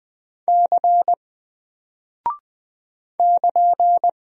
Morse code, heard as a keyed tone, about 20 wpm, 700 Hz. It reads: C <KN>
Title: Single Character (Letter:Number:Punctuation:Prosign) Alt 20wpm